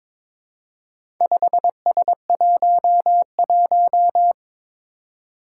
Morse code, heard as a keyed tone, about 22 wpm, 700 Hz. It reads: 5S11
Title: Four Challenging Letter-Number Combinations 22wpm